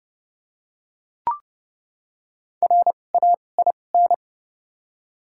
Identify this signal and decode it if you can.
Morse code, heard as a keyed tone, about 32 wpm, spaced as for 22 wpm, 700 Hz. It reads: RAIN
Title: 2x Character Spacing - Top 300 Words - 30wpm